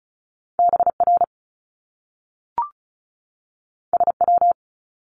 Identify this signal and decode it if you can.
Morse code, heard as a keyed tone, about 36 wpm, 700 Hz. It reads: BR SW
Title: Top Two Letter Combinations in Words 35wpm